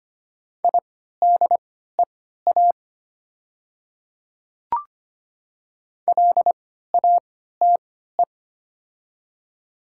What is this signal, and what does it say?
Morse code, heard as a keyed tone, about 25 wpm, spaced as for 14 wpm, 690 Hz. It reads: IDEA LATE
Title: 3x Character Spacing - Top 300 Words - 25wpm